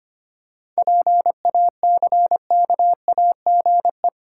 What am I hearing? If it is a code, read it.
Morse code, heard as a keyed tone, about 25 wpm, 700 Hz. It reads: PACKAGE
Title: Seven Letter Words - 25wpm